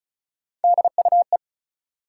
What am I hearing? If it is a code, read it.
Morse code, heard as a keyed tone, about 34 wpm, 700 Hz. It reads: DUE
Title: Three Letter Words - 35wpm